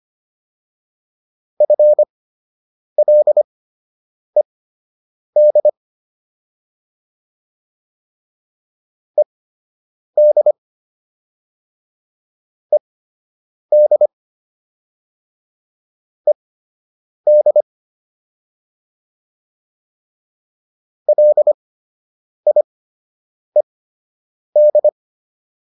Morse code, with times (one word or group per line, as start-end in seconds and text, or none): 1.6-5.7 FLED
9.2-10.5 ED
12.7-14.1 ED
16.3-17.6 ED
21.1-24.9 LIED